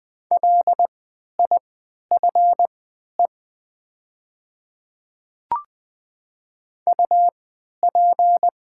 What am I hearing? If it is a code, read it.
Morse code, heard as a keyed tone, about 21 wpm, spaced as for 12 wpm, 700 Hz. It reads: LIFE UP
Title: 3x Character Spacing - Top 300 Words - 20wpm